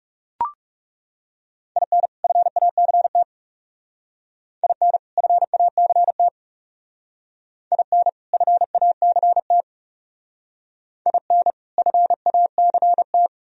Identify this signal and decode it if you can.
Morse code, heard as a keyed tone, about 36 wpm, 700 Hz. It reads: IN FACT IN FACT IN FACT IN FACT
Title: Speed-Racing - Sets of 2 Words 30wpm